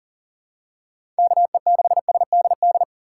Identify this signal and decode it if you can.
Morse code, heard as a keyed tone, about 39 wpm, 700 Hz. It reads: KE6SDD
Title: USA Call Signs - 3x Character Spacing - 40wpm